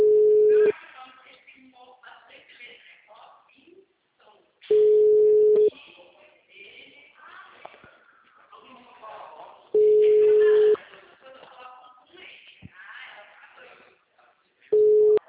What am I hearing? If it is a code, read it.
Ringback tone (Europe, 425 Hz), with speech.
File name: out- -2011- - - .2